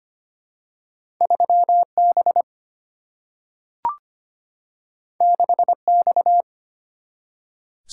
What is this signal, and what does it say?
Morse code, heard as a keyed tone, about 25 wpm, 700 Hz. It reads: 3B 6X
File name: Two Letter-Number Combinations - 25wpm